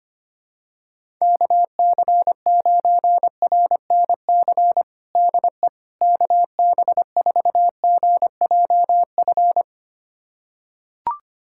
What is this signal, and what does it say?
Morse code, heard as a keyed tone, about 25 wpm, 700 Hz. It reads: KC9RNC DE KB4GJF
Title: USA Callsign DE Callsign 25wpm